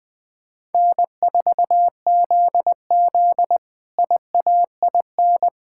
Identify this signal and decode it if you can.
Morse code, heard as a keyed tone, about 21 wpm, 700 Hz. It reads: N4ZZ IAIN
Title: K1USN SST QSO Practice - 20wpm